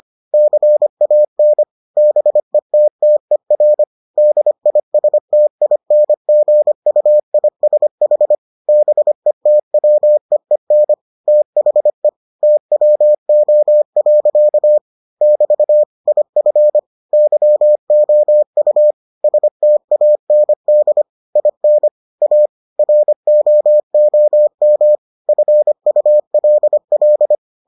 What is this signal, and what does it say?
Morse code, heard as a keyed tone, about 25 wpm, 600 Hz. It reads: CAN BETTER DISTINGUISH BETWEEN THE TWO. <BT> IF YOU STAND IN A ROOM FULL